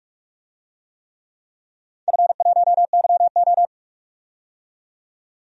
Morse code, heard as a keyed tone, about 46 wpm, 700 Hz. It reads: F1YO
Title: DXCC-100 Call Signs - 4 Characters - 45wpm